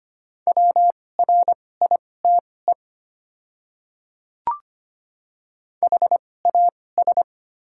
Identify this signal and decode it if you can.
Morse code, heard as a keyed tone, about 25 wpm, spaced as for 18 wpm, 700 Hz. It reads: WRITE HAS
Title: 2x Character Spacing - Top 300 Words - 25wpm